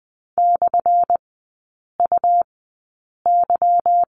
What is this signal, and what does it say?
Morse code, heard as a keyed tone, about 19 wpm, spaced as for 8 wpm, 700 Hz. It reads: /UY
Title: 205 Lesson 33e - Letter j - Three Characters 20wpm